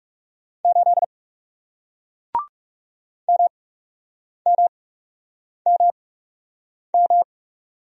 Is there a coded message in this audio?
Yes: Morse code from 0.6 to 7.2 s, heard as a keyed tone, about 40 wpm, spaced as for 8 wpm, 700 Hz. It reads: 8 MMMM